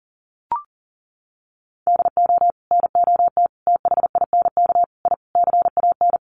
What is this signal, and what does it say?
Morse code, heard as a keyed tone, about 40 wpm, 700 Hz. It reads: DO NOT THINK I CAN
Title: Sets of 5 Words Encore2 40wpm